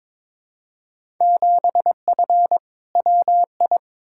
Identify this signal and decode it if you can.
Morse code, heard as a keyed tone, about 22 wpm, 700 Hz. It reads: 7F WI
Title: Field Day Exchanges Encore - 22wpm